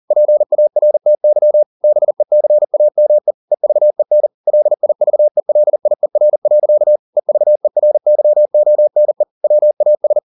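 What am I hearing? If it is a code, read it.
Morse code, heard as a keyed tone, about 40 wpm, 600 Hz. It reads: PARTY BECAME EVEN LIVELIER. EVERYONE WAS
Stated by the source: Morse code at 40 wpm, practice text, CW med Gnister.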